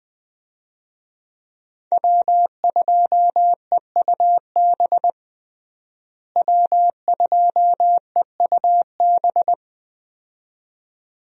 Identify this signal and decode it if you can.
Morse code, heard as a keyed tone, about 20 wpm, 700 Hz. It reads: W2EUB W2EUB
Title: USA Call Signs - 5 Characters - Repeated 2x - 20wpm